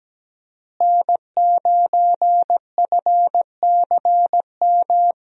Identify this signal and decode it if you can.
Morse code, heard as a keyed tone, about 17 wpm, 700 Hz. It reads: N9FCM
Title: USA Call Signs - 5 Characters - Repeated 2x - 17wpm